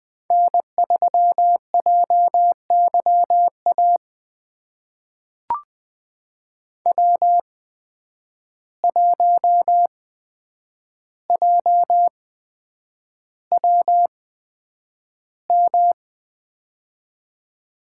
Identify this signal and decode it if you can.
Morse code, heard as a keyed tone, about 20 wpm, spaced as for 9 wpm, 700 Hz. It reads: N3JYA W1JWM
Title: USA Call Signs - 5 Characters - 8x Character Spacing - 20wpm